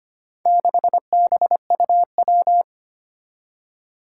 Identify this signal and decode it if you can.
Morse code, heard as a keyed tone, about 25 wpm, 700 Hz. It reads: 6BUW
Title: Four Challenging Letter-Number Combinations 25wpm